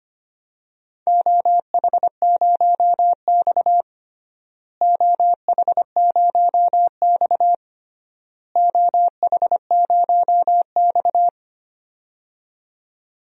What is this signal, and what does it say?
Morse code, heard as a keyed tone, about 25 wpm, 700 Hz. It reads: OH0X OH0X OH0X
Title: DXCC-100 Call Signs - 4 Characters - Repeated 3x - 25wpm